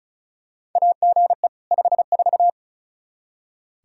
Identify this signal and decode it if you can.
Morse code, heard as a keyed tone, about 34 wpm, 700 Hz. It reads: AGE 54
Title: Practice QSO - Exchange 3 - More Info - 35wpm